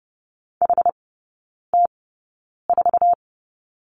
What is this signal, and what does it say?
Morse code, heard as a keyed tone, about 30 wpm, spaced as for 9 wpm, 700 Hz. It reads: HT4
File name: Lesson 17e - Letter u - Three Characters 30wpm